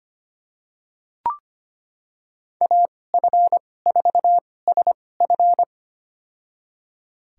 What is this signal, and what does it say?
Morse code, heard as a keyed tone, about 25 wpm, spaced as for 18 wpm, 700 Hz. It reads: AF4SF
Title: USA Call Signs - 2x Character Spacing - 25wpm